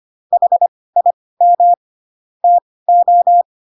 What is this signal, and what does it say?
Morse code, heard as a keyed tone, about 25 wpm, spaced as for 18 wpm, 700 Hz. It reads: HIM TO
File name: Pinocchio ch3 12 25 ik1hge